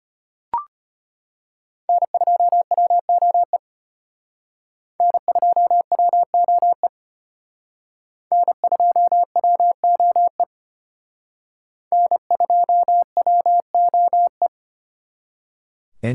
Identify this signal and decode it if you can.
Morse code, heard as a keyed tone, about 32 wpm, 700 Hz. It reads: N2WOE N2WOE N2WOE N2WOE